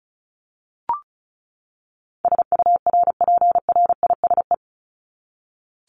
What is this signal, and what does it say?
Morse code, heard as a keyed tone, about 34 wpm, 700 Hz. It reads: SURPRISE
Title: Top 1751-2000 Words 35wpm